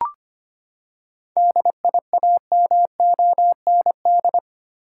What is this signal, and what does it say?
Morse code, heard as a keyed tone, about 25 wpm, 700 Hz. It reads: DIAMOND
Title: Seven Letter Words - 25wpm